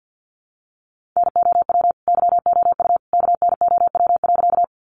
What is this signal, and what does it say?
Morse code, heard as a keyed tone, about 50 wpm, 700 Hz. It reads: NOW YOU KNOW.
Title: Sentences from Top 100 Words 50wpm